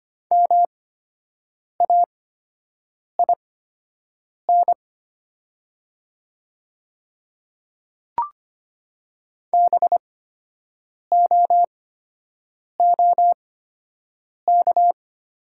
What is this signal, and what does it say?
Morse code, heard as a keyed tone, about 25 wpm, spaced as for 7 wpm, 700 Hz. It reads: MAIN BOOK
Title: 8x Character Spacing - Top 300 Words - 25wpm